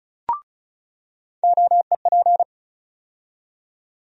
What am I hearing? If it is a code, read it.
Morse code, heard as a keyed tone, about 36 wpm, 700 Hz. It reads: OEP
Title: USA Call Sign Postfixes - 3 Letters 35wpm